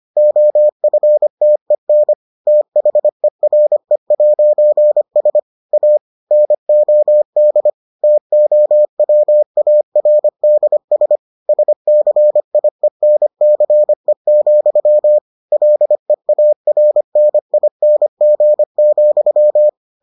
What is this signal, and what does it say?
Morse code, heard as a keyed tone, about 25 wpm, 600 Hz. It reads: OFTEN THERE'S A NOD TOWARDS SCIENCE, LEARNING,